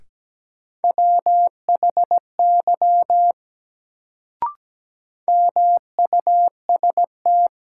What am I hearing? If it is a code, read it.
Morse code, heard as a keyed tone, about 17 wpm, 700 Hz. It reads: WHY MUST
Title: Sentences from Top 200 Words - New Word Drill - 17wpm